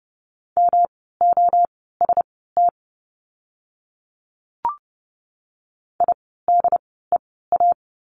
Morse code, heard as a keyed tone, about 30 wpm, spaced as for 17 wpm, 700 Hz. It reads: MOST IDEA
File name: course / 30:10wpm / 108 Lesson 18f - Letter m - Words 30wpm